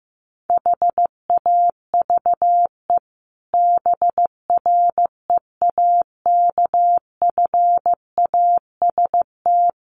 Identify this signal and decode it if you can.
Morse code, heard as a keyed tone, about 14 wpm, 700 Hz. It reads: HAVE BREAKFAST
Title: Sentences from Top 1500 Words 15wpm